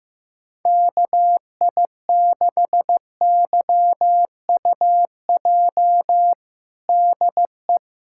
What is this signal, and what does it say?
Morse code, heard as a keyed tone, about 15 wpm, 700 Hz. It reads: KI6YUJ DE
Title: USA Callsign DE Callsign 15wpm